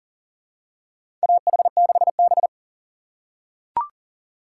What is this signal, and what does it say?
Morse code, heard as a keyed tone, about 37 wpm, 700 Hz. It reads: AH6B
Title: USA Call Signs - 4 Characters - 6x Character Spacing - 40wpm